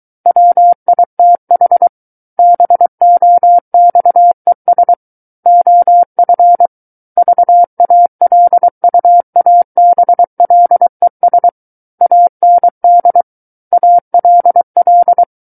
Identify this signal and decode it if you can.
Morse code, heard as a keyed tone, about 23 wpm, 700 Hz. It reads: WITH BOXES OF VALUABLES AND ALL
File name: Book One / 14-Chapter 14 - 23 WPM